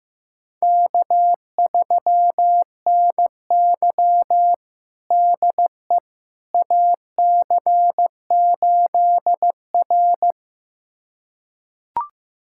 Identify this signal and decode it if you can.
Morse code, heard as a keyed tone, about 15 wpm, 700 Hz. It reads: K3NY DE AC8R
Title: USA Callsign DE Callsign - 4 Characters - 15wpm